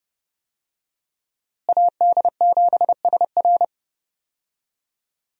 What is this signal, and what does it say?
Morse code, heard as a keyed tone, about 32 wpm, 700 Hz. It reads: AD7SR
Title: USA Call Signs - 5 Characters - Repeated 3x - 30wpm